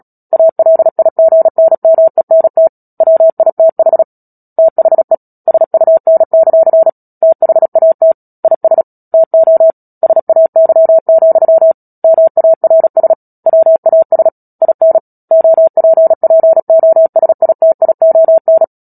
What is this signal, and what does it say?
Morse code, heard as a keyed tone, about 37 wpm, 650 Hz. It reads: ALIGNMENT WITH THE SUN; THAT IS TO SAY, MARS WAS IN OPPOSITION
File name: Book Two / 10-Chapter 10 - Epilogue - 36 WPM